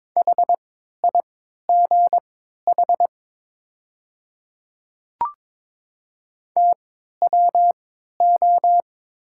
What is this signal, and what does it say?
Morse code, heard as a keyed tone, about 23 wpm, spaced as for 13 wpm, 690 Hz. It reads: HIGH TWO
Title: 3x Character Spacing - Top 300 Words - 22wpm